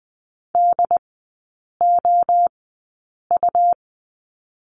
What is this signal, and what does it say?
Morse code, heard as a keyed tone, about 20 wpm, spaced as for 8 wpm, 700 Hz. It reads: DOU
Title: 138 Lesson 23e - Letter f - Three Characters 20wpm